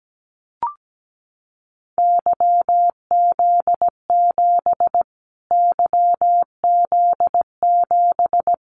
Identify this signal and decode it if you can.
Morse code, heard as a keyed tone, about 17 wpm, 700 Hz. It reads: YZ7 YZ7
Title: Callsign Prefixes for Top 25 DXCC - Repeated 3x - 17wpm